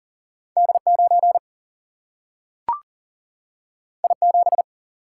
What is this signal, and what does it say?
Morse code, heard as a keyed tone, about 40 wpm, 700 Hz. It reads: D 9 I 7
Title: Single-Letter Single-Number 40wpm